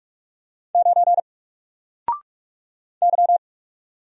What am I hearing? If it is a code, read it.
Morse code, heard as a keyed tone, about 44 wpm, 700 Hz. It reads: 9 Y
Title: Single Character (Letter:Number:Punctuation:Prosign) 45wpm